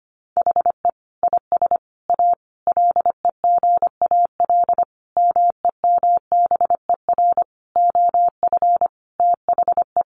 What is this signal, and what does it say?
Morse code, heard as a keyed tone, about 25 wpm, 700 Hz. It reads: HE IS A LEGAL MEMBER OF THE
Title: Sentences from Top 1200 Words 25wpm